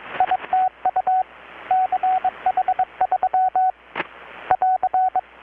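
Morse code, heard as a keyed tone, about 22 wpm, 750 Hz, with radio noise. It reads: ITU CH3 <AR>
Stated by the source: Morse from KPH's Night of Nights XX maritime radiotelegraphy event, received in Chiapas, Mexico about 3,800 km away on a Kenwood R-600 with a grounded 10 m long-wire antenna.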